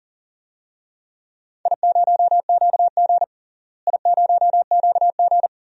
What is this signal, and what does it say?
Morse code, heard as a keyed tone, about 40 wpm, 700 Hz. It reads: I0QG I0QG
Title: DXCC-100 Call Signs - 4 Characters - Repeated 2x - 40wpm